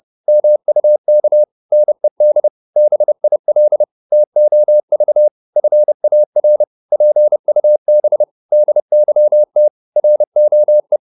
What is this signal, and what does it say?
Morse code, heard as a keyed tone, about 30 wpm, 600 Hz. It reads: MUK NED BIL TOV FAR PUB DYT ROE
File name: Korte ord DK 30wpm